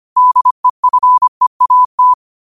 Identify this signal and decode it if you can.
Morse code, heard as a keyed tone, about 25 wpm, 1000 Hz. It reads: DEFEAT
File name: qotdfiles / 25wpm